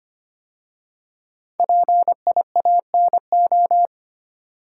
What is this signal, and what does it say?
Morse code, heard as a keyed tone, about 25 wpm, 700 Hz. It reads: PIANO
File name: 5 Letter Words - First 3 Letters Given - 25wpm